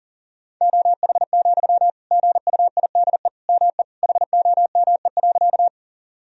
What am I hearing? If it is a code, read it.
Morse code, heard as a keyed tone, about 40 wpm, 700 Hz. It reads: OH, GUIDE ME HOME.
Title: Sentences from Top 1000 Words - Review Part 3 - 40wpm